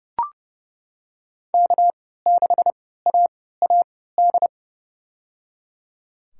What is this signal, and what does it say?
Morse code, heard as a keyed tone, about 30 wpm, spaced as for 17 wpm, 700 Hz. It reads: K6AAD
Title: USA Call Signs - 5 Characters - 3x Character Spacing - 30wpm